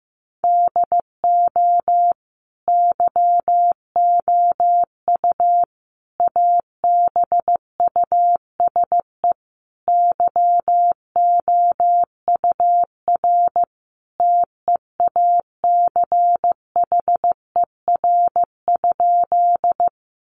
Morse code, heard as a keyed tone, about 15 wpm, 700 Hz. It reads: DO YOU ABUSE YOUR TEACHER?
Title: Sentences from Top 1300 Words 15wpm